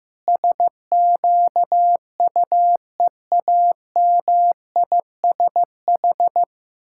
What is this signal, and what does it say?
Morse code, heard as a keyed tone, about 16 wpm, 700 Hz. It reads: SQUEAMISH
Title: Pangram Challenge v2 - 15wpm